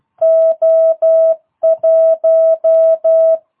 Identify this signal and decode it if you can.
Morse code, heard as a keyed tone, about 12 wpm, 650 Hz. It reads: O1
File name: jh8cbh / 2022cw